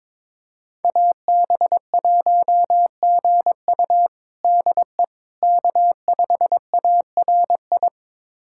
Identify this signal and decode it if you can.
Morse code, heard as a keyed tone, about 21 wpm, 700 Hz. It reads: AB1GU DE K5ARI
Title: USA Callsign DE Callsign - 5 Characters - 22wpm